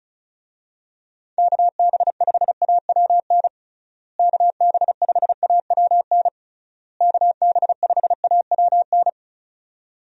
Morse code, heard as a keyed tone, about 34 wpm, 700 Hz. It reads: KB5AWN KB5AWN KB5AWN
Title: USA Call Signs - 6 Characters - Repeated 3x - 35wpm